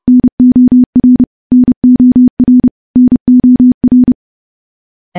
Morse code, heard as a keyed tone, about 30 wpm, 250 Hz. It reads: NOR NOR NOR
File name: morse / morse-training-30WPM